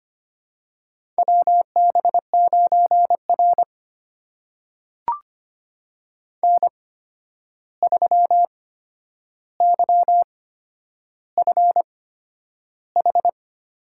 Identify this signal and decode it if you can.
Morse code, heard as a keyed tone, about 25 wpm, spaced as for 10 wpm, 700 Hz. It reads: WB9R N3YFH
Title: USA Call Signs - 8x Character Spacing - 25wpm